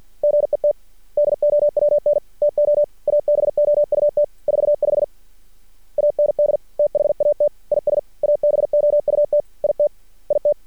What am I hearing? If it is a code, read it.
Morse code, heard as a keyed tone, about 53 wpm, 590 Hz. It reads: GET DOWN TO ABOUT 45 AND THAT IS ABOUT IT IT
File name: archived / CRSnet